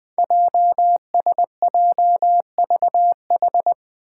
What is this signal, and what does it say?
Morse code, heard as a keyed tone, about 21 wpm, 700 Hz. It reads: JSJVH
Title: Five Challenging Letter-Number Combinations 20wpm